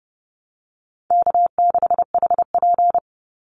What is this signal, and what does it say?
Morse code, heard as a keyed tone, about 30 wpm, 700 Hz. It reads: K6HP
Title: Easy Call Signs Repeated 3x - 30wpm